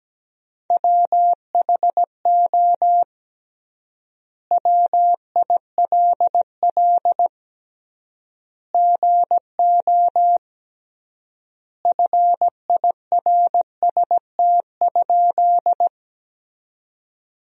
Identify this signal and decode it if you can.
Morse code, heard as a keyed tone, about 17 wpm, 700 Hz. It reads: WHO WILL GO FIRST?
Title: 3x Word-Spacing - Sentences from Top 100 Words - 17wpm